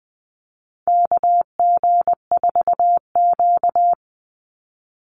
Morse code, heard as a keyed tone, about 20 wpm, 700 Hz. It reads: KG4Q
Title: Call Signs - 20wpm